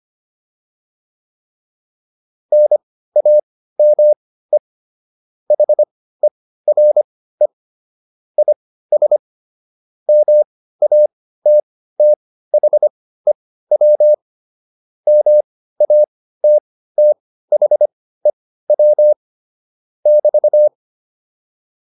Morse code, heard as a keyed tone, about 25 wpm, spaced as for 15 wpm, 600 Hz. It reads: NAME HERE IS MATTHEW MATTHEW <BT>